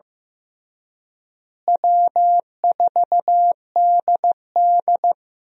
Morse code, heard as a keyed tone, about 15 wpm, 700 Hz. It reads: W4DD
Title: USA Call Signs - 4 Characters - Repeated 3x - 15wpm